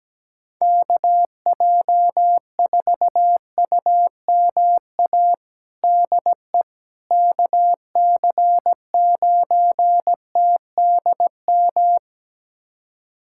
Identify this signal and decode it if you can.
Morse code, heard as a keyed tone, about 17 wpm, 700 Hz. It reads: KJ4UMA DE KC9TDM
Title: USA Callsign DE Callsign - 6 Characters - 17wpm